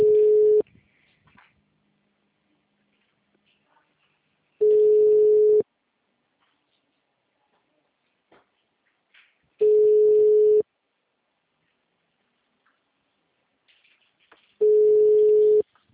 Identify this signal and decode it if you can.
Ringback tone (Europe, 425 Hz).